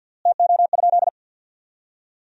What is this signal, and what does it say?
Morse code, heard as a keyed tone, about 50 wpm, 700 Hz. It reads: TO?
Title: 3x Word-Spacing - Sentences from Top 100 Words - 50wpm